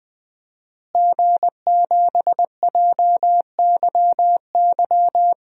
Morse code, heard as a keyed tone, about 20 wpm, 700 Hz. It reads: G7JYY
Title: DXCC-100 Call Signs - 5 Characters - Repeated 2x - 20wpm